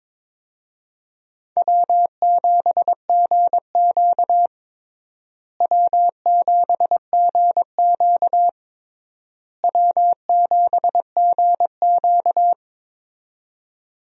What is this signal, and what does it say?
Morse code, heard as a keyed tone, about 21 wpm, 700 Hz. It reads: W7GQ W7GQ W7GQ
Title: USA Call Signs - 4 Characters - Repeated 3x - 22wpm